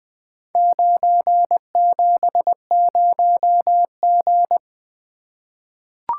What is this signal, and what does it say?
Morse code, heard as a keyed tone, about 20 wpm, 700 Hz. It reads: 970G
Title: Four Letter-Number Combinations - 20wpm